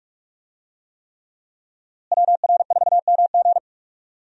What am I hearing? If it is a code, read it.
Morse code, heard as a keyed tone, about 46 wpm, 700 Hz. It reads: WR4MG
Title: USA Call Signs - 5 Characters - Repeated 3x - 45wpm